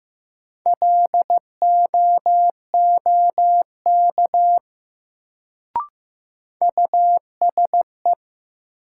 Morse code, heard as a keyed tone, about 15 wpm, 700 Hz. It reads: LOOK USE
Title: Sentences from Top 100 Words - New Word Drill - 15wpm